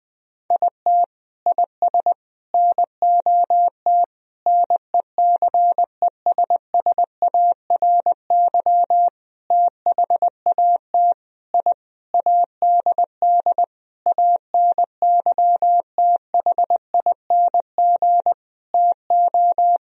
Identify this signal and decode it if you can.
Morse code, heard as a keyed tone, about 20 wpm, 700 Hz. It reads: IT IS NOT NECESSARY THAT I ADD ANYTHING TO
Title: Sentences from Top 1000 Words - Review Part 1 - 20wpm